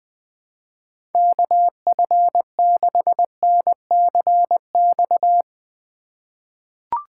Morse code, heard as a keyed tone, about 21 wpm, 700 Hz. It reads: KF6NCX
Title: K1USN SST QSO Practice - 20wpm